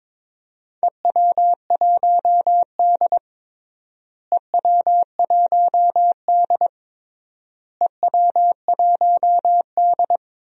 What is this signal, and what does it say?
Morse code, heard as a keyed tone, about 22 wpm, 700 Hz. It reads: EW1D EW1D EW1D
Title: DXCC-100 Call Signs - 4 Characters - Repeated 3x - 22wpm